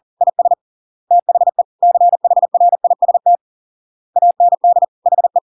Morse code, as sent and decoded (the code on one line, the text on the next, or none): .. ... / - .... / -.-. .... .-. .. ... - / .- -. -.. / .... .
IS THE CHRIST AND HE